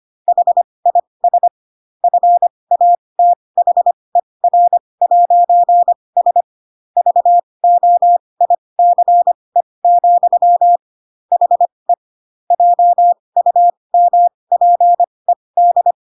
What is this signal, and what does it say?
Morse code, heard as a keyed tone, about 25 wpm, spaced as for 20 wpm, 700 Hz. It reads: HIS FATHER'S VOICE, HE JUMPED
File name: Pinocchio / Pinocchio ch7 15 25 ik1hge